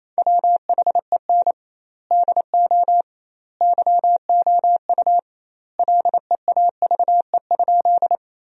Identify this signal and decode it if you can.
Morse code, heard as a keyed tone, about 28 wpm, 700 Hz. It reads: WHEN DO YOU LEAVE?
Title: 2x Word-Spacing - Sentences from Top 400 Words - 28wpm